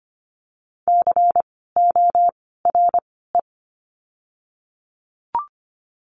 Morse code, heard as a keyed tone, about 25 wpm, 700 Hz. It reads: C O R E
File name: course / 25:10wpm / 094 Lesson 16f - Letter c - Words 25wpm